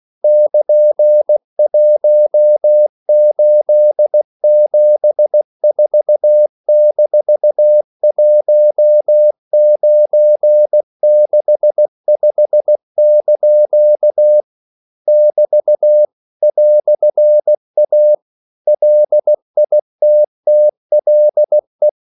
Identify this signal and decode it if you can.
Morse code, heard as a keyed tone, about 16 wpm, 600 Hz. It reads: <KN>1874-1965) <BT> "A LITTLE
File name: CW / Practice / EnglishQuotations0216